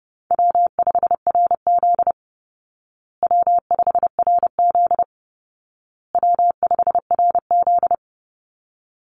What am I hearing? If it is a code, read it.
Morse code, heard as a keyed tone, about 28 wpm, 700 Hz. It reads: W5RZ W5RZ W5RZ
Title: Easy Call Signs Repeated 3x - 30wpm